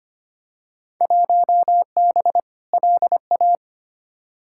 Morse code, heard as a keyed tone, about 25 wpm, 700 Hz. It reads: 1B LA
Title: Field Day Exchanges Encore - 25wpm